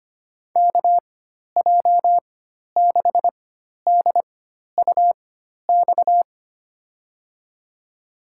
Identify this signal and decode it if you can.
Morse code, heard as a keyed tone, about 25 wpm, spaced as for 12 wpm, 700 Hz. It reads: KJ6DUX
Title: USA Call Signs - 6 Characters - 4x Character Spacing - 25wpm